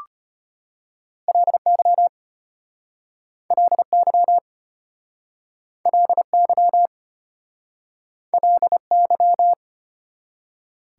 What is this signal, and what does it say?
Morse code, heard as a keyed tone, about 36 wpm, 700 Hz. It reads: LY LY LY LY